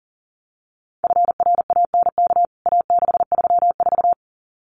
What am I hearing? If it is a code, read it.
Morse code, heard as a keyed tone, about 40 wpm, 700 Hz. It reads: FRANK A634
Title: CWT Exchange 40wpm